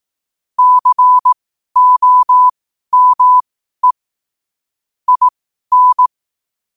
Morse code, heard as a keyed tone, about 18 wpm, spaced as for 12 wpm, 1000 Hz. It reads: COME IN
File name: qotdfiles / 12wpm